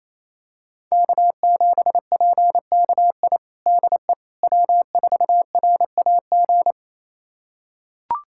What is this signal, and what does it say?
Morse code, heard as a keyed tone, about 28 wpm, 700 Hz. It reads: K7PKI DE W4RAG
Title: USA Callsign DE Callsign - 5 Characters - 28wpm